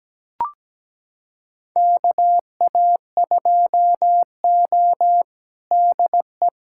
Morse code, heard as a keyed tone, about 17 wpm, 700 Hz. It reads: KA2O DE
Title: USA Callsign DE Callsign - 4 Characters - 17wpm